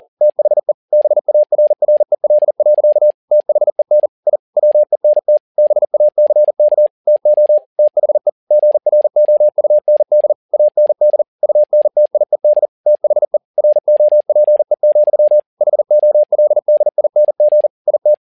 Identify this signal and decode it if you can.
Morse code, heard as a keyed tone, about 40 wpm, 600 Hz. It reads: THE BARREL. THEN I WENT BACK TO THE GROUND AND UNTIED THE ROPE, HOLDING IT
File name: learning center / tower trouble / tower touble 40wpm